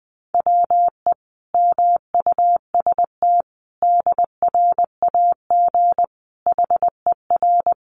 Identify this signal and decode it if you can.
Morse code, heard as a keyed tone, about 20 wpm, 700 Hz. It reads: WE MUST DRAG HER
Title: Sentences from Top 1500 Words 20wpm